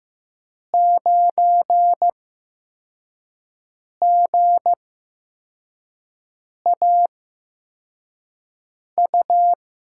Morse code, heard as a keyed tone, about 15 wpm, spaced as for 4 wpm, 700 Hz. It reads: 9GAU
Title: USA Call Signs - 6 Characters - 8x Character Spacing - 15wpm